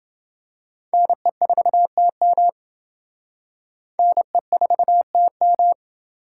Morse code, heard as a keyed tone, about 27 wpm, 700 Hz. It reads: NE4TM NE4TM